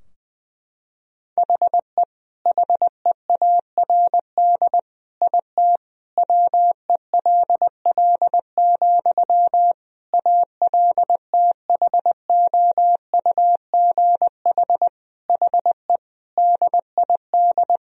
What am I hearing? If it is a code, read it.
Morse code, heard as a keyed tone, about 20 wpm, 700 Hz. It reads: HE HEARD IT WELL, ALTHOUGH HE DID
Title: Sentences from Top 1000 Words - Review Part 3 - 20wpm